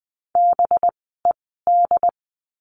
Morse code, heard as a keyed tone, about 20 wpm, spaced as for 14 wpm, 700 Hz. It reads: BED
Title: 185 Lesson 30f - Letter b - Words 20wpm